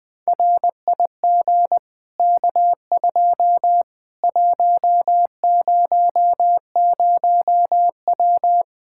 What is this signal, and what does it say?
Morse code, heard as a keyed tone, about 20 wpm, 700 Hz. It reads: RIG K2 100W
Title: Practice QSO - Exchange 2 - Info - 20wpm